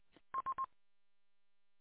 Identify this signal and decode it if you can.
There is no code.